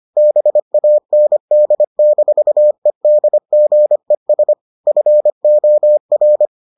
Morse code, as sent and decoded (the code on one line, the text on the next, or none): -... .- -. -.. -....- . -.. --. ... / ..-. --- .-.
BAND-EDGES FOR